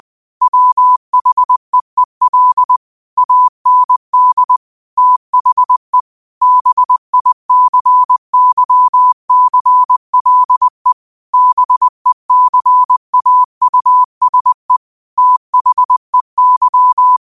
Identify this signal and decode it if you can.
Morse code, heard as a keyed tone, about 20 wpm, 1000 Hz. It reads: WHEEL AND THE BICYCLE BECAUSE THEY
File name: qotdfiles / 20wpm